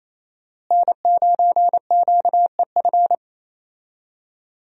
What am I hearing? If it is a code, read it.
Morse code, heard as a keyed tone, about 28 wpm, 700 Hz. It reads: N9QEF
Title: USA Call Signs - 5 Characters - Repeated 2x - 28wpm